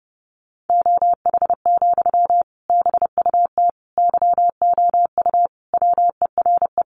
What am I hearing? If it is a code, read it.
Morse code, heard as a keyed tone, about 30 wpm, 700 Hz. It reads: OH, BUT YOU WERE
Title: Sentences from Top 500 Words 30wpm